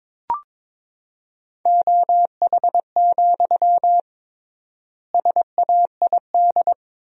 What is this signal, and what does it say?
Morse code, heard as a keyed tone, about 22 wpm, 700 Hz. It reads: OH, SAID
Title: 3x Word-Spacing - Sentences from Top 500 Words - 22wpm